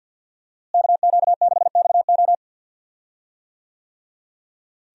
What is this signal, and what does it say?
Morse code, heard as a keyed tone, about 45 wpm, 700 Hz. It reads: KQ6XO
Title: USA Call Signs - 5 Characters - Repeated 2x - 50wpm